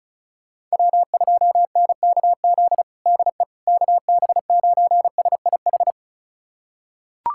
Morse code, heard as a keyed tone, about 36 wpm, 700 Hz. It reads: W2NKZ DE KB9SIH
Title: USA Callsign DE Callsign 35wpm